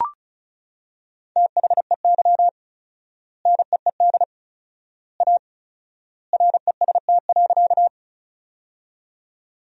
Morse code, heard as a keyed tone, about 36 wpm, 700 Hz. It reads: THEY NEED A REST.
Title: 4x Word-Spacing - Sentences from Top 500 Words - 35wpm